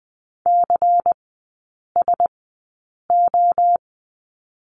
Morse code, heard as a keyed tone, about 20 wpm, spaced as for 8 wpm, 700 Hz. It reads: CSO
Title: 093 Lesson 16e - Letter c - Three Characters 20wpm